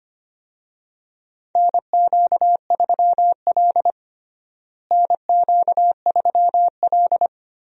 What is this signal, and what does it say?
Morse code, heard as a keyed tone, about 25 wpm, 700 Hz. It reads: NQ3L NQ3L
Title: USA Call Signs - 4 Characters - Repeated 2x - 25wpm